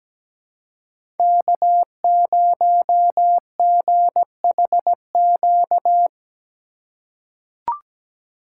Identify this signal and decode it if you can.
Morse code, heard as a keyed tone, about 17 wpm, 700 Hz. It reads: K0GHQ
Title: USA Call Signs - 5 Characters - 4x Character Spacing - 17wpm